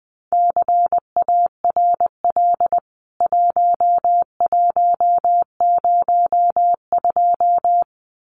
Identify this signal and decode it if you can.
Morse code, heard as a keyed tone, about 19 wpm, 700 Hz. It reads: CARL 1102
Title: CWT Exchange 20wpm